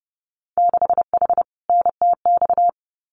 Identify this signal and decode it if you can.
Morse code, heard as a keyed tone, about 30 wpm, 700 Hz. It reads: 6H NTX
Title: Winter Field Day Exchanges Encore 30wpm